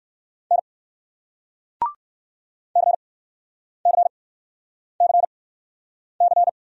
Morse code, heard as a keyed tone, about 58 wpm, spaced as for 9 wpm, 700 Hz. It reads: N CCCC